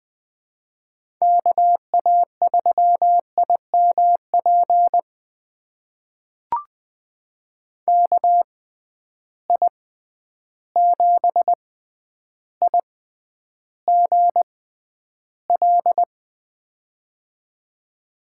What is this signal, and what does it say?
Morse code, heard as a keyed tone, about 20 wpm, spaced as for 10 wpm, 700 Hz. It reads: KA3IMP KI7IGL
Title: USA Call Signs - 6x Character Spacing - 20wpm